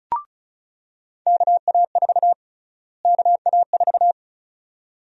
Morse code, heard as a keyed tone, about 36 wpm, 700 Hz. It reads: KA4 KA4
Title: USA Call Sign Prefixes - Repeated 2x - 35wpm